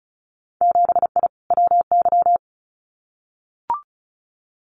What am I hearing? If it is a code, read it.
Morse code, heard as a keyed tone, about 34 wpm, 700 Hz. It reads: 7I WY
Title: Winter Field Day Exchanges Encore 35wpm